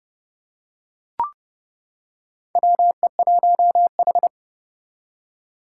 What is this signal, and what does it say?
Morse code, heard as a keyed tone, about 29 wpm, 700 Hz. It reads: WE1H
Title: Four Challenging Letter-Number Combinations 30wpm